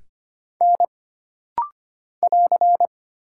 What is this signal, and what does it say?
Morse code, heard as a keyed tone, about 25 wpm, 700 Hz. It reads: N <AR>
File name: Single Character (Letter:Number:Punctuation:Prosign) - Rapid-Fire - 25wpm